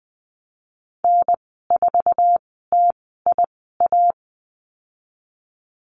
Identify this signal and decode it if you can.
Morse code, heard as a keyed tone, about 20 wpm, spaced as for 15 wpm, 700 Hz. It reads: N4TIA
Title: 048 Lesson 9g - Number 4 - Call Signs 20wpm